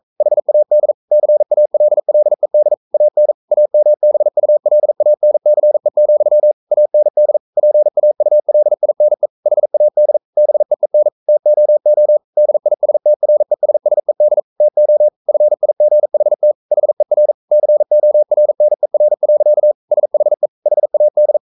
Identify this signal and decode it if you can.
Morse code, heard as a keyed tone, about 42 wpm, 600 Hz. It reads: HAD CALLED AN AMBULANCE, AND PAULINE HAD BEEN TOO DISTRESSED TO FIGHT HER CORNER. SHE HAD
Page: Romanen Never af Ken Follet på engelsk.